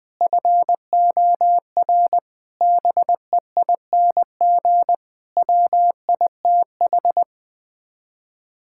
Morse code, heard as a keyed tone, about 20 wpm, 700 Hz. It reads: FOR BEING WITH
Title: Sets of 9 Words - 20wpm